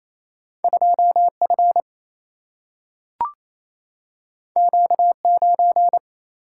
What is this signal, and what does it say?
Morse code, heard as a keyed tone, about 28 wpm, 700 Hz. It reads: 2F Q9
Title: Single-Letter Single-Number 28wpm